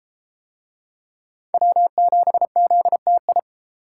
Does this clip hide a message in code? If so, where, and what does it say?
Yes: Morse code from 1.5 to 3.4 s, heard as a keyed tone, about 31 wpm, 700 Hz. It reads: W7ZTI